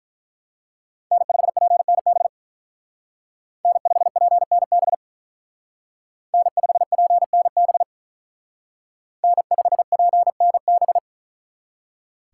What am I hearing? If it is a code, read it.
Morse code, heard as a keyed tone, about 43 wpm, 700 Hz. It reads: N5PNB N5PNB N5PNB N5PNB
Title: Speed Racing - USA 1x3 Call Signs 35wpm